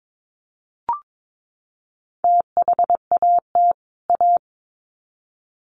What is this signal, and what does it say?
Morse code, heard as a keyed tone, about 22 wpm, 700 Hz. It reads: THAT A
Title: Sets of 2 Words 22wpm